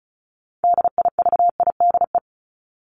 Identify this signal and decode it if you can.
Morse code, heard as a keyed tone, about 34 wpm, 700 Hz. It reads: DIVIDE
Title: ED Suffix - Top 100 Words with Base Words 35wpm